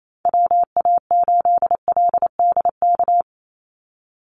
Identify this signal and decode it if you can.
Morse code, heard as a keyed tone, about 28 wpm, 700 Hz. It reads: WA8LDK
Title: USA Call Signs 28wpm